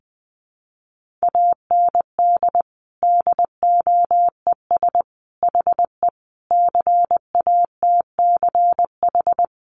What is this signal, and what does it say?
Morse code, heard as a keyed tone, about 20 wpm, 700 Hz. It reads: AND DOES HE CATCH
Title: Sentences from Top 1500 Words 20wpm